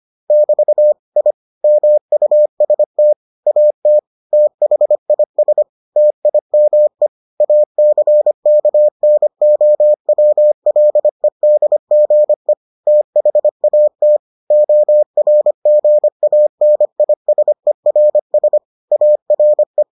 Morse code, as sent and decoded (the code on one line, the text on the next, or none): -...- / .. / -- ..- ... - / .- - / - .... .. ... / - .. -- . / .- -.-. -.- -. --- .-- .-.. -.. --. / - .... .- - / --- .-. --. .- -. .. ... .-. ... / .- .-. .
<BT> I MUST AT THIS TIME ACKNOWLEDGE THAT ORGANISERS ARE